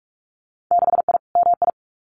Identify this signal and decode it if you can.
Morse code, heard as a keyed tone, about 46 wpm, 690 Hz. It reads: 6I MI
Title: Winter Field Day Exchanges Encore 45wpm